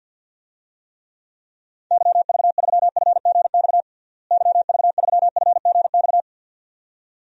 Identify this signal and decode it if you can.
Morse code, heard as a keyed tone, about 50 wpm, 700 Hz. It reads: YV3FGX YV3FGX
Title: DXCC-100 Call Signs - 6 Characters - Repeated 2x - 50wpm